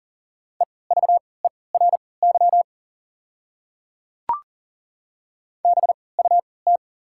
Morse code, heard as a keyed tone, about 40 wpm, spaced as for 23 wpm, 690 Hz. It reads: EVERY BUT
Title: 3x Character Spacing - Top 300 Words - 40wpm